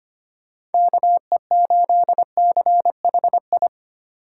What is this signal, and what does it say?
Morse code, heard as a keyed tone, about 25 wpm, 700 Hz. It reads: KE8CHI
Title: USA Call Signs - 6 Characters - 3x Character Spacing - 25wpm